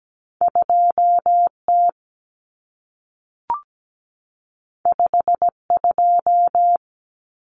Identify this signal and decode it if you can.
Morse code, heard as a keyed tone, about 17 wpm, 700 Hz. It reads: 2T 52
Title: Common Cut Numbers - 2-Digit 17wpm